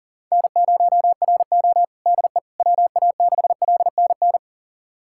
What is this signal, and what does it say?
Morse code, heard as a keyed tone, about 40 wpm, 700 Hz. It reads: N0RO DE WA6LNN
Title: USA Callsign DE Callsign 40wpm